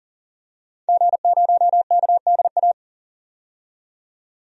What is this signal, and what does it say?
Morse code, heard as a keyed tone, about 40 wpm, 700 Hz. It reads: G0KDA
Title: DXCC-100 Call Signs - 5 Characters - 40wpm